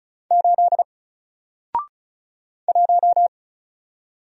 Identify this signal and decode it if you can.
Morse code, heard as a keyed tone, about 36 wpm, 700 Hz. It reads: 8 1
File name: Single-Letter-Number-35wpm